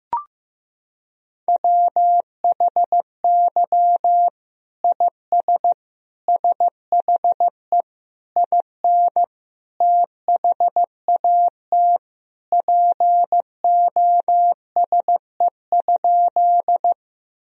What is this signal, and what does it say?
Morse code, heard as a keyed tone, about 15 wpm, 700 Hz. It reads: WHY IS SHE IN THAT POSE?
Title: Sentences from Top 1000 Words - Review Part 2 - 15wpm